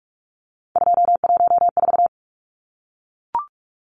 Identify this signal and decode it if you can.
Morse code, heard as a keyed tone, about 44 wpm, 700 Hz. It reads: 214
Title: Numbers - 3-Digit 45wpm